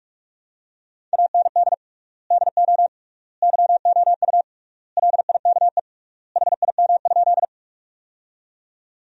Morse code, heard as a keyed tone, about 41 wpm, 700 Hz. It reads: AND DO YOU LIKE HIM?
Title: 3x Word-Spacing - Sentences from Top 100 Words - 45wpm